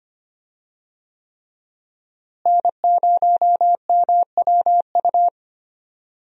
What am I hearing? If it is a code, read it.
Morse code, heard as a keyed tone, about 25 wpm, 700 Hz. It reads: N0MWU
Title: USA Call Signs - 5 Characters - Repeated 3x - 25wpm